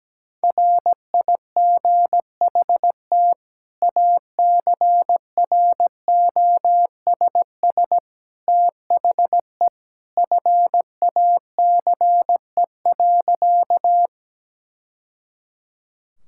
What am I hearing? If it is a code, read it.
Morse code, heard as a keyed tone, about 17 wpm, 700 Hz. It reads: RIGHT ACROSS THE FACE.
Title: Sentences from Top 1000 Words - Review Part 5 - 17wpm